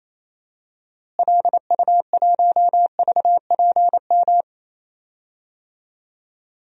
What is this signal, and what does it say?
Morse code, heard as a keyed tone, about 28 wpm, 700 Hz. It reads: LU1VPM
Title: DXCC-100 Call Signs - 6 Characters - Repeated 2x - 28wpm